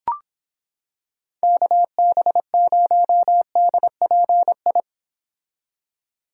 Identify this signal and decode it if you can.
Morse code, heard as a keyed tone, about 26 wpm, 700 Hz. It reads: KB0DPI